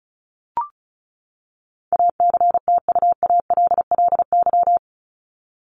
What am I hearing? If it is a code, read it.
Morse code, heard as a keyed tone, about 34 wpm, 700 Hz. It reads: ACTUALLY
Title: LY Suffix - Top 100 Words with Base Words 35wpm